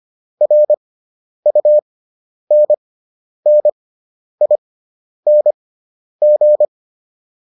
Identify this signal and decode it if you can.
Morse code, heard as a keyed tone, about 25 wpm, spaced as for 10 wpm, 600 Hz. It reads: RUNNING
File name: QSO409 10